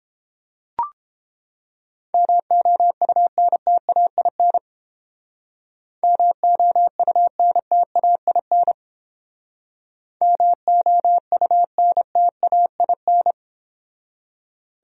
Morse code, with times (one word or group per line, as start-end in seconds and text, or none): 2.1-4.6 MOUNTAIN
6.0-8.7 MOUNTAIN
10.2-13.3 MOUNTAIN